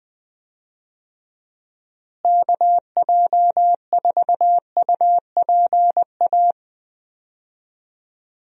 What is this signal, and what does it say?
Morse code, heard as a keyed tone, about 20 wpm, 700 Hz. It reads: KJ4UPA
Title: USA Call Signs - 6 Characters - Repeated 2x - 20wpm